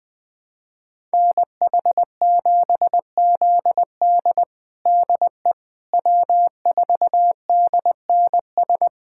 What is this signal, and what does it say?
Morse code, heard as a keyed tone, about 21 wpm, 700 Hz. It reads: NH7ZD DE W4DNS
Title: USA Callsign DE Callsign - 5 Characters - 20wpm